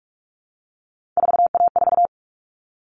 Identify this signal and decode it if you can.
Morse code, heard as a keyed tone, about 44 wpm, 700 Hz. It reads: 4A4
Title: All Cut Numbers - 3-Digit 45wpm